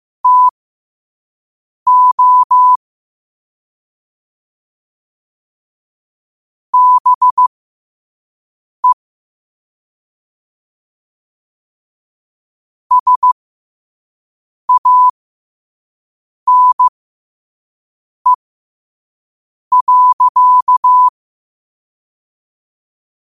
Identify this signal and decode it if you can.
Morse code, heard as a keyed tone, about 15 wpm, spaced as for 5 wpm, 1000 Hz. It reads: TO BE SANE.
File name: qotdfiles / 5wpm